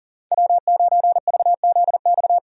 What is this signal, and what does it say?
Morse code, heard as a keyed tone, about 38 wpm, 700 Hz. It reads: W9VZX
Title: USA Call Signs - 5 Characters - 40wpm